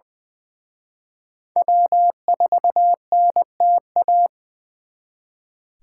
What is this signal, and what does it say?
Morse code, heard as a keyed tone, about 24 wpm, 700 Hz. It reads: W4NTA
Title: Speed Racing - USA Call Signs 20wpm